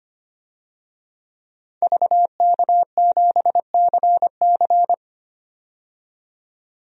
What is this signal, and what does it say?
Morse code, heard as a keyed tone, about 25 wpm, 700 Hz. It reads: VK7CC
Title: DXCC-100 Call Signs - 5 Characters - 25wpm